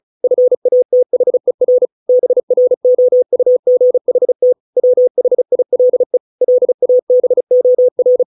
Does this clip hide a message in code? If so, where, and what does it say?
Yes: Morse code from 0.2 to 8.2 s, heard as a keyed tone, about 34 wpm, 500 Hz. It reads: FATHER BROUGHT WHILE LABOR